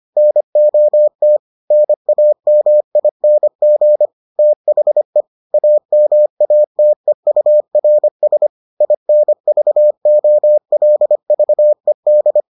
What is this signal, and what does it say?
Morse code, heard as a keyed tone, about 25 wpm, 600 Hz. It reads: NOT NAMING THE AMATEURS INVOLVED